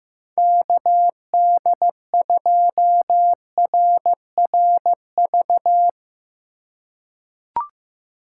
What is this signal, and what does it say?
Morse code, heard as a keyed tone, about 15 wpm, 700 Hz. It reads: KD2RRV
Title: USA Call Signs - 6 Characters - 4x Character Spacing - 15wpm